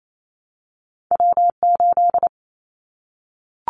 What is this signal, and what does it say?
Morse code, heard as a keyed tone, about 28 wpm, 700 Hz. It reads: W8
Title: Callsign Prefixes for Top 25 DXCC - Repeated 3x - 28wpm